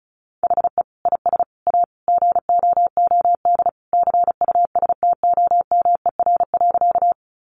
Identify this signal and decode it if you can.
Morse code, heard as a keyed tone, about 34 wpm, 700 Hz. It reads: HE IS A GOOD CUSTOMER.
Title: Sentences from Top 1100 Words 35wpm